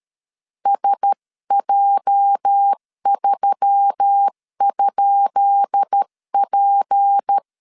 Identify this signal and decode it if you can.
Morse code, heard as a keyed tone, about 13 wpm, 800 Hz. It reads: SJ3?P
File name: LEZIONI / 10 SIGNS / 10B INT 12wpm 2414